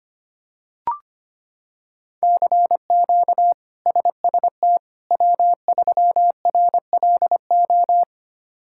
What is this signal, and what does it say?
Morse code, heard as a keyed tone, about 25 wpm, 700 Hz. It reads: CQ SST W3RLO
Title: K1USN SST QSO Practice - 25wpm